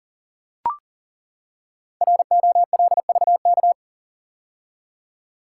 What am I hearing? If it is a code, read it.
Morse code, heard as a keyed tone, about 39 wpm, 700 Hz. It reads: ROLVK
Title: Code Groups - 5 Letter:Numbers - 40wpm